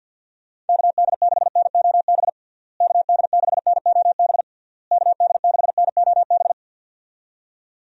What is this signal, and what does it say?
Morse code, heard as a keyed tone, about 50 wpm, 700 Hz. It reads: KD6NOB KD6NOB KD6NOB
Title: USA Call Signs - 6 Characters - Repeated 3x - 50wpm